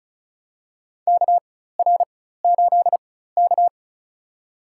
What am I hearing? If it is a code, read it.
Morse code, heard as a keyed tone, about 34 wpm, spaced as for 16 wpm, 700 Hz. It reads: KR8K
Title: USA Call Signs - 4 Characters - 4x Character Spacing - 35wpm